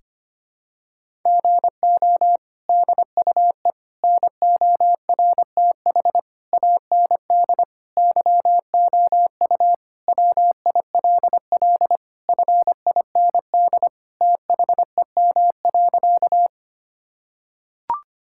Morse code, heard as a keyed tone, about 25 wpm, 700 Hz. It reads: GO DUE NORTH AND YOU WILL FIND THEM.
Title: Sentences from Top 1000 Words - Review Part 6 - 25wpm